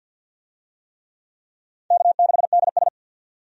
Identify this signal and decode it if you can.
Morse code, heard as a keyed tone, about 50 wpm, 700 Hz. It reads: K6DS
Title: USA Call Signs - 4 Characters - 50wpm